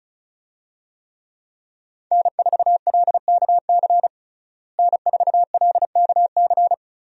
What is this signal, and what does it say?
Morse code, heard as a keyed tone, about 36 wpm, 700 Hz. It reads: N4LKC N4LKC
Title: USA Call Signs - 5 Characters - Repeated 2x - 35wpm